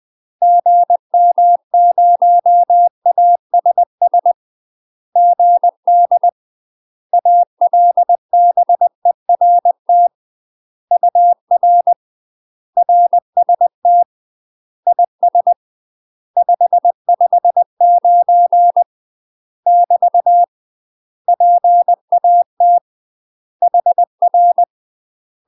Morse code, heard as a keyed tone, about 20 wpm, 700 Hz. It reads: GM0ASS GD ALBERT UR RST IS 559 <BT> PAT HR
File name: UK MASTER QSOs TRAINING / 20WPM / UK MASTER QSOs 0062